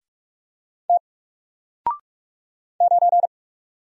Morse code, heard as a keyed tone, about 44 wpm, 700 Hz. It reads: T 9
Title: Single Letter-Number Alt 45wpm